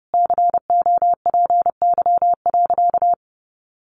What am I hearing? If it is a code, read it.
Morse code, heard as a keyed tone, about 30 wpm, 700 Hz. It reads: COPY.
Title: Sentences from Top 1400 Words 30wpm